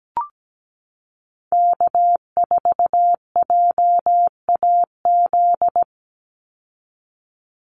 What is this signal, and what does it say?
Morse code, heard as a keyed tone, about 17 wpm, 700 Hz. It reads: K4JAZ
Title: Call Signs Repeated 3x - 17wpm